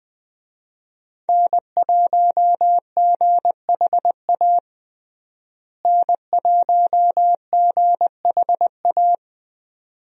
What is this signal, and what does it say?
Morse code, heard as a keyed tone, about 20 wpm, 700 Hz. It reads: N1GHA N1GHA
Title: USA Call Signs - 5 Characters - Repeated 2x - 20wpm